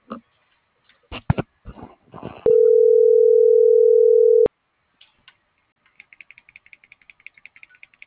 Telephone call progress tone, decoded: ringback tone

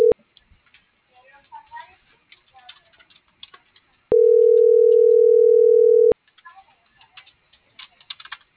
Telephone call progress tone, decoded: ringback tone

